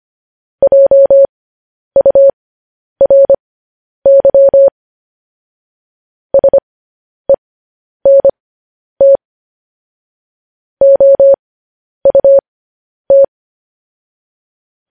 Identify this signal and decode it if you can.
Morse code, heard as a keyed tone, about 25 wpm, spaced as for 10 wpm, 550 Hz. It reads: JURY SENT OUT
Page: news-headlines-in-morse-code-at-10-wpm